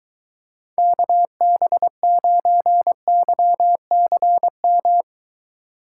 Morse code, heard as a keyed tone, about 23 wpm, 700 Hz. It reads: KB9YCM